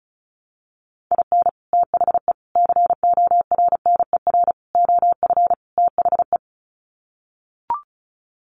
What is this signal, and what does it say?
Morse code, heard as a keyed tone, about 34 wpm, 700 Hz. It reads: IN THE CORNER OF THE
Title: Sets of 5 Words Encore 35wpm